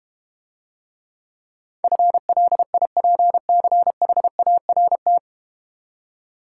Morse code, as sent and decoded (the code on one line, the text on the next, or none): ..-. .-.. .. .--. -.-. .... .- .-. -
FLIPCHART